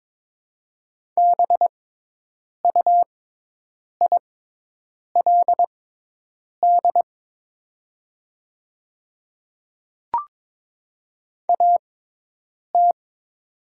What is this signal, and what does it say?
Morse code, heard as a keyed tone, about 22 wpm, spaced as for 8 wpm, 700 Hz. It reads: BUILD AT
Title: 6x Character Spacing - Top 200 Words - 22wpm